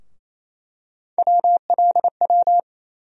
Morse code, heard as a keyed tone, about 28 wpm, 700 Hz. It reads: WLW
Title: USA Call Sign Postfixes - 3 Letters 28wpm